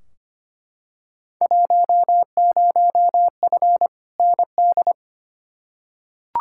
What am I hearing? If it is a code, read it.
Morse code, heard as a keyed tone, about 25 wpm, 700 Hz. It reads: 10F ND
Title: Field Day Exchanges Encore - 25wpm